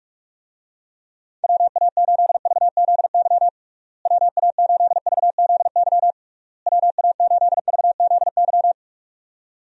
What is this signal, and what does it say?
Morse code, heard as a keyed tone, about 46 wpm, 700 Hz. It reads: WA8VZY WA8VZY WA8VZY
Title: USA Call Signs - 6 Characters - Repeated 3x - 45wpm